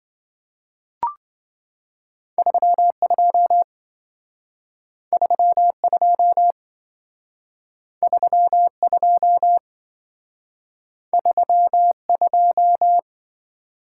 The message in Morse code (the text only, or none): 32 32 32 32